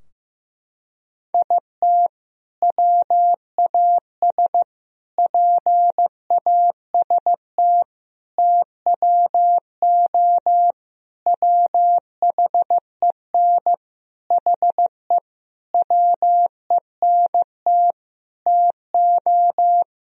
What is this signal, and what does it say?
Morse code, heard as a keyed tone, about 15 wpm, 700 Hz. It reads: IT WAS PAST TWO WHEN HE WENT TO
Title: Sentences from Top 1000 Words - Review Part 3 - 15wpm